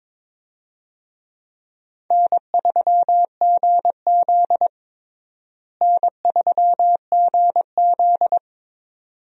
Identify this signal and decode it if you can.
Morse code, heard as a keyed tone, about 22 wpm, 700 Hz. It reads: N3GZ N3GZ
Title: USA Call Signs - 4 Characters - Repeated 2x - 22wpm